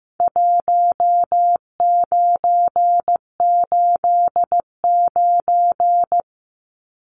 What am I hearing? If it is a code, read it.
Morse code, heard as a keyed tone, about 15 wpm, 700 Hz. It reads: 1989
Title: Numbers - 4-Digit 15wpm